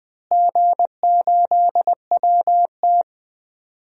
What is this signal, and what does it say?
Morse code, heard as a keyed tone, about 20 wpm, 700 Hz. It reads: G8WT
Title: DXCC-100 Call Signs - 4 Characters - Repeated 2x - 20wpm